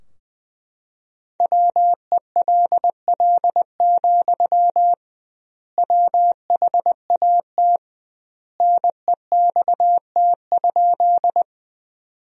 Morse code, heard as a keyed tone, about 20 wpm, 700 Hz. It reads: WELL, WHAT NEXT?
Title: 2x Word-Spacing - Sentences from Top 300 Words - 20wpm